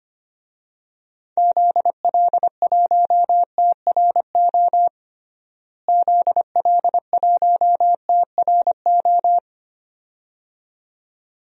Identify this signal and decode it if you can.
Morse code, heard as a keyed tone, about 25 wpm, 700 Hz. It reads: ZL1TRO ZL1TRO
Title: DXCC-100 Call Signs - 6 Characters - Repeated 2x - 25wpm